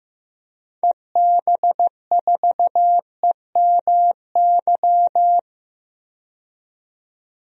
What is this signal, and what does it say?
Morse code, heard as a keyed tone, about 15 wpm, 700 Hz. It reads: EB4EMY
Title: DXCC-100 Call Signs - 6 Characters - 15wpm